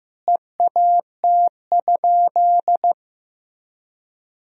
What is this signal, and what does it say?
Morse code, heard as a keyed tone, about 15 wpm, 700 Hz. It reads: EAT?
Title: Sentences from Top 300 Words and Abbreviations 15wpm